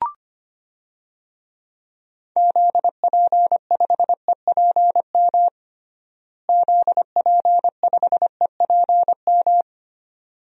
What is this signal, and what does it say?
Morse code, heard as a keyed tone, about 25 wpm, 700 Hz. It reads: ZP5EPM ZP5EPM
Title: DXCC-100 Call Signs - 6 Characters - Repeated 2x - 25wpm